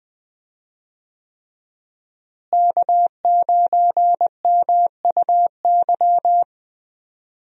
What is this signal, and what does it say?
Morse code, heard as a keyed tone, about 20 wpm, 700 Hz. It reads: K9MUY
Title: USA Call Signs - 5 Characters - Repeated 3x - 20wpm